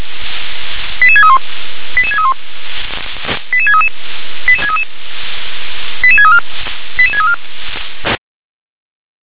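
Five-tone selective call (ZVEI), heard as quick runs of five tones, tones of about 70 ms. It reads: 88631 88631 88633 88633 88634 88634
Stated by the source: Five-tone ZVEI selective call received by SDR.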